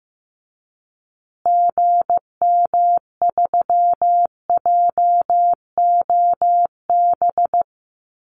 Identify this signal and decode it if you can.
Morse code, heard as a keyed tone, about 15 wpm, 700 Hz. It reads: GM3JOB
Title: Call Signs Repeated 2x - 15wpm